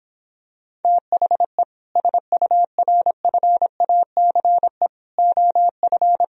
Morse code, heard as a keyed tone, about 26 wpm, 700 Hz. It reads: THE SURFACE OF